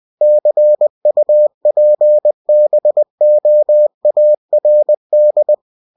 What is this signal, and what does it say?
Morse code, heard as a keyed tone, about 20 wpm, 600 Hz. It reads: CUPBOARD
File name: cw-books / winnie-the-pooh / winnie-20wpm-page-0049